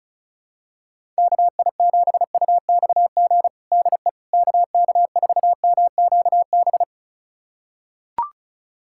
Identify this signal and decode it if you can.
Morse code, heard as a keyed tone, about 36 wpm, 700 Hz. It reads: KI7UXG DE KK4MQB
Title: USA Callsign DE Callsign - 6 Characters - 35wpm